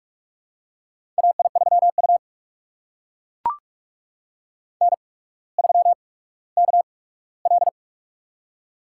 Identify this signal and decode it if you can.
Morse code, heard as a keyed tone, about 44 wpm, spaced as for 19 wpm, 700 Hz. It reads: AI3U N3KL
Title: USA Call Signs - 8x Character Spacing - 45wpm